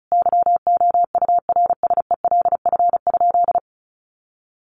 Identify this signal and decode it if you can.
Morse code, heard as a keyed tone, about 34 wpm, 700 Hz. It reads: YOURSELF?
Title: Sentences from Top 1200 Words 35wpm